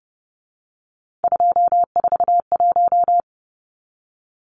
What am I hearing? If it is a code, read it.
Morse code, heard as a keyed tone, about 30 wpm, 700 Hz. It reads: 241
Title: Numbers - 3-Digit 30wpm